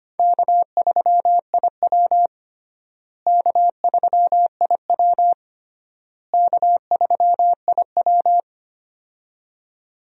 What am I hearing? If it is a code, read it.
Morse code, heard as a keyed tone, about 25 wpm, 700 Hz. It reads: K3IW K3IW K3IW
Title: USA Call Signs - 4 Characters - Repeated 3x - 25wpm